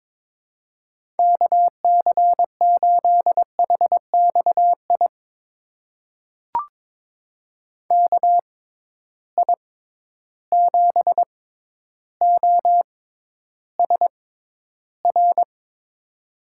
Morse code, heard as a keyed tone, about 22 wpm, spaced as for 11 wpm, 700 Hz. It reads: KC8HXI KI7OSR
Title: USA Call Signs - 6 Characters - 6x Character Spacing - 22wpm